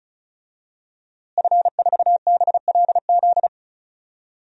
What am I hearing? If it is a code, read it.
Morse code, heard as a keyed tone, about 36 wpm, 700 Hz. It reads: F4BLZ
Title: DXCC-100 Call Signs - 5 Characters - Repeated 3x - 35wpm